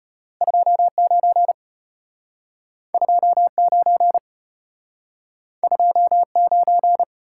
Morse code, heard as a keyed tone, about 33 wpm, 700 Hz. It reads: 29 29 29